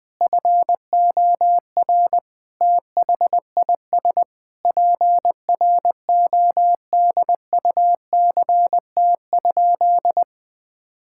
Morse code, heard as a keyed tone, about 20 wpm, 700 Hz. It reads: FOR THIS PRODUCT?
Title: Sentences with 9-Word Phrases - 20wpm